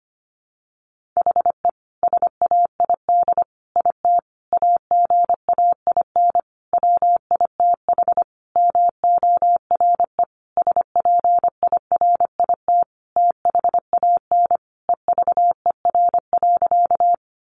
Morse code, heard as a keyed tone, about 25 wpm, 700 Hz. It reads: HE SAID IT AGAIN WITH MORE SPIRIT THAN EVER.
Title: Sentences from Top 1400 Words 25wpm